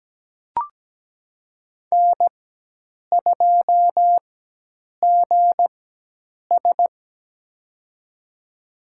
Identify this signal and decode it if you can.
Morse code, heard as a keyed tone, about 17 wpm, spaced as for 8 wpm, 700 Hz. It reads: N2GS
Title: USA Call Signs - 4 Characters - 4x Character Spacing - 17wpm